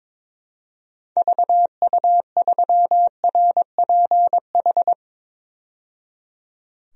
Morse code, heard as a keyed tone, about 22 wpm, 700 Hz. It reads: VU3RPH
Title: DXCC-100 Call Signs - 6 Characters - 22wpm